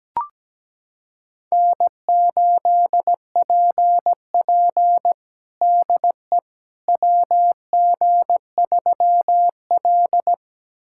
Morse code, heard as a keyed tone, about 17 wpm, 700 Hz. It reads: N8PP DE WG3L
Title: USA Callsign DE Callsign - 4 Characters - 17wpm